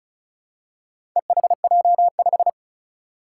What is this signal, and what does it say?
Morse code, heard as a keyed tone, about 36 wpm, 700 Hz. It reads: EHJ5
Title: Four Challenging Letter-Number Combinations 35wpm